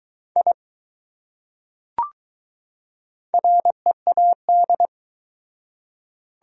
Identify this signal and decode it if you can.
Morse code, heard as a keyed tone, about 20 wpm, 700 Hz. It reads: I READ